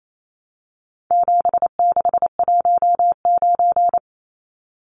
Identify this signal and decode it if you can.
Morse code, heard as a keyed tone, about 28 wpm, 700 Hz. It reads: 7619
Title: Numbers - 4-Digit 28wpm